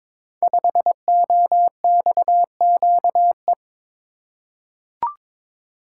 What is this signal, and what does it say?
Morse code, heard as a keyed tone, about 22 wpm, 690 Hz. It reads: 5OXQE
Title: Code Groups - 5 Letter:Numbers - 22wpm